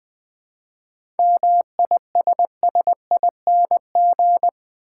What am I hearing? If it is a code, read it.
Morse code, heard as a keyed tone, about 20 wpm, 690 Hz. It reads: MISSING
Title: 7 Letter Words - First 3 Letters Given - 20wpm